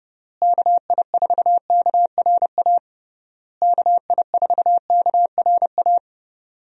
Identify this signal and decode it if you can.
Morse code, heard as a keyed tone, about 30 wpm, 700 Hz. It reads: KI4KRA KI4KRA
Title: USA Call Signs - 6 Characters - Repeated 3x - 30wpm